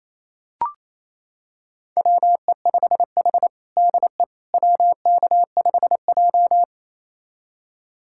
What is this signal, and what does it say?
Morse code, heard as a keyed tone, about 28 wpm, 700 Hz. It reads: WE5H DE WK5J
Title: USA Callsign DE Callsign - 4 Characters - 28wpm